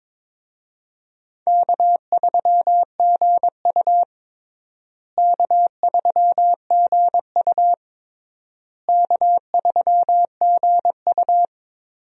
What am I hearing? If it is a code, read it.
Morse code, heard as a keyed tone, about 22 wpm, 700 Hz. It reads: K3GU K3GU K3GU
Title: USA Call Signs - 4 Characters - Repeated 3x - 22wpm